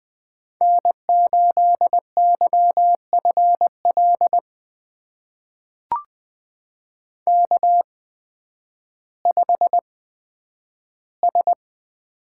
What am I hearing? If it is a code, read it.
Morse code, heard as a keyed tone, about 20 wpm, spaced as for 11 wpm, 700 Hz. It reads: N8YFL K5S
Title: USA Call Signs - 5 Characters - 8x Character Spacing - 20wpm